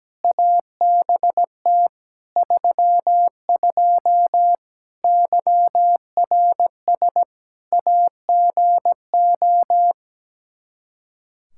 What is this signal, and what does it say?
Morse code, heard as a keyed tone, about 17 wpm, 700 Hz. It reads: ABT 32 YRS AGO
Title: Practice QSO - Exchange 3 - More Info - 17wpm